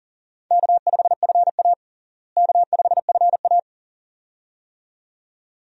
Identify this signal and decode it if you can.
Morse code, heard as a keyed tone, about 40 wpm, 700 Hz. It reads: K5FA K5FA
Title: USA Call Signs - 4 Characters - Repeated 2x - 40wpm